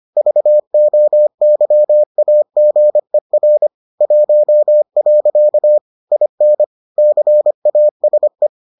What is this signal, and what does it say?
Morse code, heard as a keyed tone, about 25 wpm, 600 Hz. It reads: VOYAGER 1. IN CASE